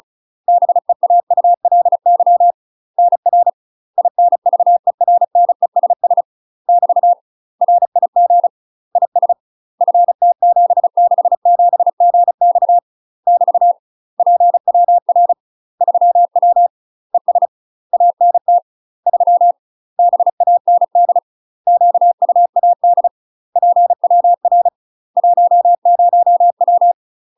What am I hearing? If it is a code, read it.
Morse code, heard as a keyed tone, about 36 wpm, 700 Hz. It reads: BEAULY NR INVERNESS <BT> RIG IS FT767GX <BT> PWR 3W ES ANT 3 BAND QUAD PWR 10W
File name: UK MASTER QSOs TRAINING / 35WPM / UK MASTER QSOs 0036